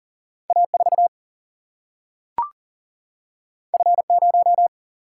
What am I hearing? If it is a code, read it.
Morse code, heard as a keyed tone, about 39 wpm, 700 Hz. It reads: A4 F0
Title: Single-Letter Single-Number 40wpm